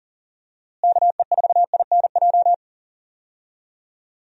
Morse code, heard as a keyed tone, about 40 wpm, 700 Hz. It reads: KE4INJ
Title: USA Call Signs - 6 Characters - Repeated 3x - 40wpm